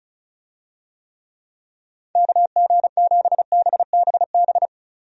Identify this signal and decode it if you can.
Morse code, heard as a keyed tone, about 34 wpm, 700 Hz. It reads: KG7BBB
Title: USA Call Signs - 6 Characters - Repeated 2x - 35wpm